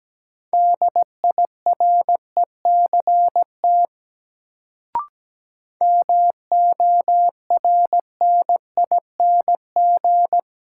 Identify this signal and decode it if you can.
Morse code, heard as a keyed tone, about 17 wpm, 700 Hz. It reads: DIRECT MORNING
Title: Sentences from Top 400 Words - New Word Drill - 17wpm